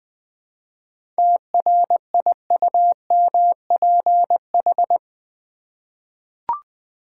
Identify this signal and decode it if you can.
Morse code, heard as a keyed tone, about 20 wpm, 700 Hz. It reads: TRIUMPH
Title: 7 Letter Words - First 3 Letters Given - 20wpm